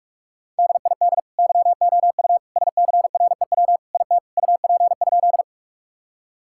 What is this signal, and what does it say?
Morse code, heard as a keyed tone, about 46 wpm, 700 Hz. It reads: DID YOU SCREW IT UP?
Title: Sentences from Top 1600 Words 45wpm